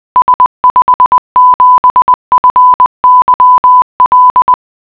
Morse code, heard as a keyed tone, about 20 wpm, 1000 Hz. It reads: S57FYL